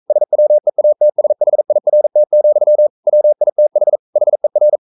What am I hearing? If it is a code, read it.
Morse code, heard as a keyed tone, about 42 wpm, 600 Hz. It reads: SWEATSHIRT, WITH HER
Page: Romanen Never af Ken Follet på engelsk.